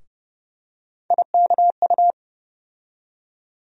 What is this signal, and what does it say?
Morse code, heard as a keyed tone, about 30 wpm, 700 Hz. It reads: IKU
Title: USA Call Sign Postfixes - 3 Letters 30wpm